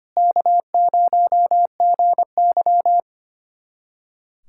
Morse code, heard as a keyed tone, about 25 wpm, 700 Hz. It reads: K0GY
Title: USA Callsign DE Callsign - 4 Characters - 25wpm